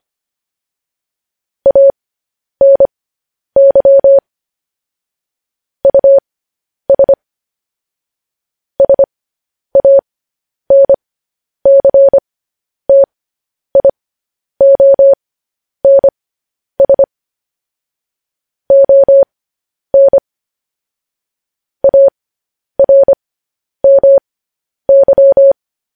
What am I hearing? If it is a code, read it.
Morse code, heard as a keyed tone, about 25 wpm, spaced as for 10 wpm, 550 Hz. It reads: ANY US SANCTIONS ON ARMY